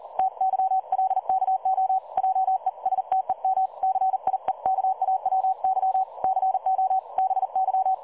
Morse code, heard as a keyed tone, about 40 wpm, 760 Hz. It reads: TYLKO JESTEM CIEKAWY CO BY
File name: dlazaawansowanych / nagraniazpasma / websdr recording 3565.0kHz